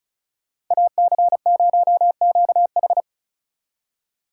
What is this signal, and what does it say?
Morse code, heard as a keyed tone, about 42 wpm, 700 Hz. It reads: AC0QH
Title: Speed Racing - USA 2x2 Call Signs 35wpm